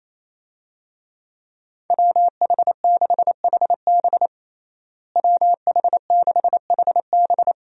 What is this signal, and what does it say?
Morse code, heard as a keyed tone, about 27 wpm, 700 Hz. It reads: WH6HB WH6HB
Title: USA Call Signs - 5 Characters - Repeated 3x - 28wpm